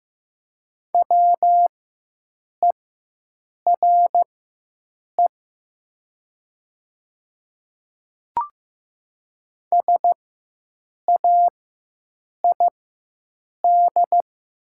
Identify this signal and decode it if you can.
Morse code, heard as a keyed tone, about 15 wpm, spaced as for 7 wpm, 700 Hz. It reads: WERE SAID
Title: 4x Character Spacing - Top 100 Words - 15wpm